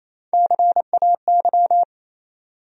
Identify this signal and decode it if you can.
Morse code, heard as a keyed tone, about 28 wpm, 700 Hz. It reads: CAY
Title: USA Call Sign Postfixes - 3 Letters 28wpm